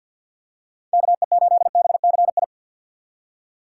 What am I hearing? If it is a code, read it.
Morse code, heard as a keyed tone, about 47 wpm, 700 Hz. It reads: KE8BCI
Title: USA Call Signs - 6 Characters - 6x Character Spacing - 50wpm